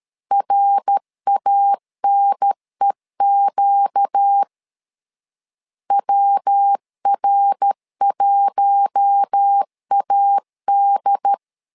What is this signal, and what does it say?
Morse code, heard as a keyed tone, about 13 wpm, 800 Hz. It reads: RANEQ WR1AD
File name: LEZIONI / 4 Q1 12wpm 2028